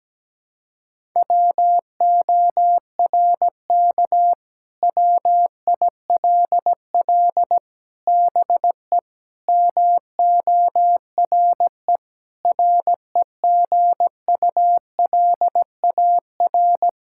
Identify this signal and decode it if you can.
Morse code, heard as a keyed tone, about 17 wpm, 700 Hz. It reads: WORK WILL BE MORE REGULAR
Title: Sentences from Top 1000 Words - Review Part 6 - 17wpm